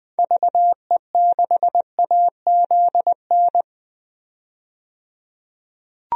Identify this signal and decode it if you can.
Morse code, heard as a keyed tone, about 20 wpm, 700 Hz. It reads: VE6AZN
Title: DXCC-100 Call Signs - 6 Characters - 20wpm